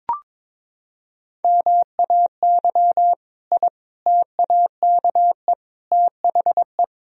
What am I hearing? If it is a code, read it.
Morse code, heard as a keyed tone, about 22 wpm, 700 Hz. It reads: MAY I TAKE THE
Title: Sentences from Top 1000 Words - Review Part 1 - 22wpm